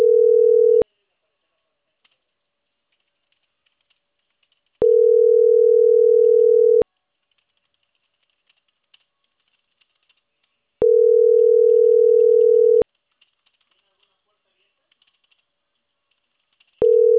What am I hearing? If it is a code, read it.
Ringback tone (North America), 440 + 480 Hz.